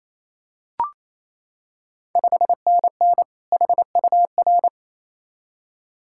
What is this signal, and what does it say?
Morse code, heard as a keyed tone, about 27 wpm, 700 Hz. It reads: 5NN HUR
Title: Ontario QSO Party Exchange 28wpm